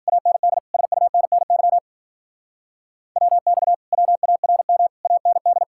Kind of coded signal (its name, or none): Morse code